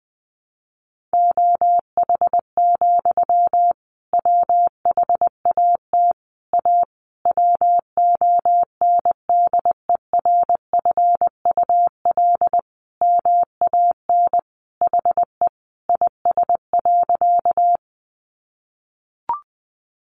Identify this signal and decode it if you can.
Morse code, heard as a keyed tone, about 20 wpm, 700 Hz. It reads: OH, WHAT A WONDERFUL MAN HE IS.
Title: Sentences from Top 1100 Words 20wpm